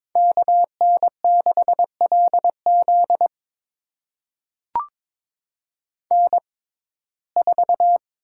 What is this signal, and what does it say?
Morse code, heard as a keyed tone, about 21 wpm, spaced as for 16 wpm, 700 Hz. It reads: KN6LZ N4
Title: USA Call Signs - 5 Characters - 6x Character Spacing - 22wpm